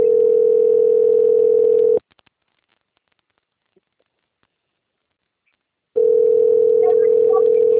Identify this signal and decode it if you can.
Ringback tone (North America), 440 + 480 Hz.
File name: out- - - - - .20705